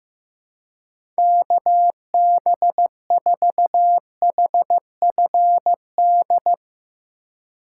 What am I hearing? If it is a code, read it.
Morse code, heard as a keyed tone, about 15 wpm, 700 Hz. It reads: KB4HFD
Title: USA Call Signs - 6 Characters - 4x Character Spacing - 15wpm